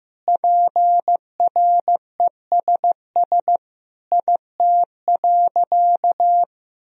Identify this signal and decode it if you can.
Morse code, heard as a keyed tone, about 16 wpm, 700 Hz. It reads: PRESS IT.
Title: Sentences from Top 1000 Words - Review Part 1 - 15wpm